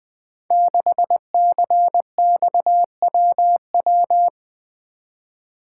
Morse code, heard as a keyed tone, about 20 wpm, 700 Hz. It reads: 6CXWW
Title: Code Groups - 5 Letter:Numbers - 20wpm